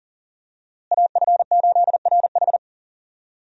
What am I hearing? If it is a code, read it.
Morse code, heard as a keyed tone, about 37 wpm, 700 Hz. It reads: AF8RH
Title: USA Call Signs - 5 Characters - Repeated 2x - 40wpm